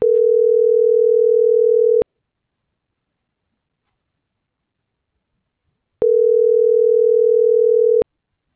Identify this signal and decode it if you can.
Ringback tone (North America), 440 + 480 Hz.